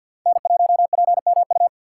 Morse code, heard as a keyed tone, about 50 wpm, 700 Hz. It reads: N1PMU
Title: USA Call Signs - 5 Characters - 2x Character Spacing - 50wpm